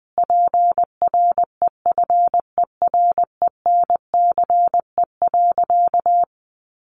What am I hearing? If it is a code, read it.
Morse code, heard as a keyed tone, about 20 wpm, 700 Hz. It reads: PREFERENCE.
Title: Sentences from Top 1500 Words 20wpm